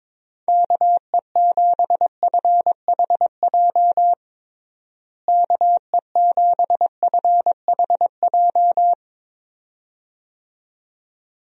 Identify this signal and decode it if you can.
Morse code, heard as a keyed tone, about 22 wpm, 700 Hz. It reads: KE7FHJ KE7FHJ
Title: USA Call Signs - 6 Characters - Repeated 2x - 22wpm